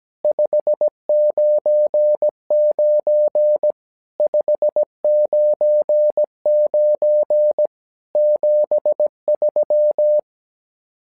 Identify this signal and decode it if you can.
Morse code, heard as a keyed tone, about 17 wpm, 600 Hz. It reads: 599 599 73
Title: SOTA QSO Practice 17wpm